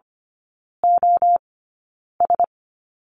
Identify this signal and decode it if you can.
Morse code, heard as a keyed tone, about 25 wpm, spaced as for 9 wpm, 700 Hz. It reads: OS
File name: Lesson 17e - Letter u - Three Characters 25wpm